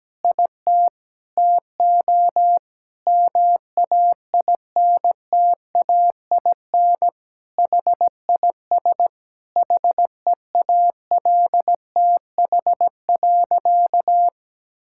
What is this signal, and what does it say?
Morse code, heard as a keyed tone, about 17 wpm, 700 Hz. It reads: IT TO MAINTAIN HIS HEALTH.
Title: Sentences from Top 1000 Words - Review Part 5 - 17wpm